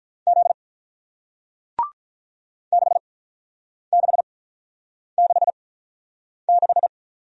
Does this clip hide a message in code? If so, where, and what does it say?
Yes: Morse code from 0.3 to 6.9 s, heard as a keyed tone, about 48 wpm, spaced as for 9 wpm, 700 Hz. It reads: Z 6666